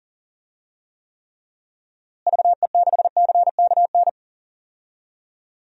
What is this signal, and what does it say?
Morse code, heard as a keyed tone, about 40 wpm, 700 Hz. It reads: VE6CKN
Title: DXCC-100 Call Signs - 6 Characters - Repeated 3x - 40wpm